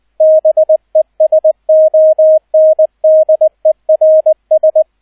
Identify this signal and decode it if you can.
Morse code, heard as a keyed tone, about 20 wpm, 630 Hz, with radio noise. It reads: BESONDERS
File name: 002 Zweiter Streich 20wpm